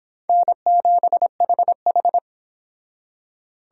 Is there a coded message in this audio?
Yes: Morse code from 0.3 to 2.2 s, heard as a keyed tone, about 25 wpm, 700 Hz. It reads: N7HH